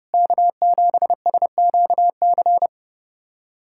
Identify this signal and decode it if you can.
Morse code, heard as a keyed tone, about 30 wpm, 700 Hz. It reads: K7SQC
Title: USA Callsign DE Callsign 30wpm